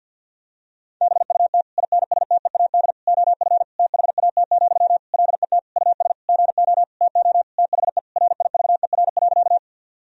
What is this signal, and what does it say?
Morse code, heard as a keyed tone, about 50 wpm, 700 Hz. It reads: BUT INSTEAD OF THAT, LET US GO TO THE RIVER.
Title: Sentences from Top 1000 Words - Review Part 4 - 50wpm